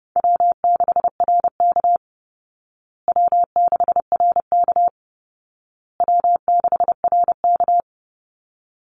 Morse code, heard as a keyed tone, about 30 wpm, 700 Hz. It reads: W6RK W6RK W6RK
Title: Easy Call Signs Repeated 3x - 30wpm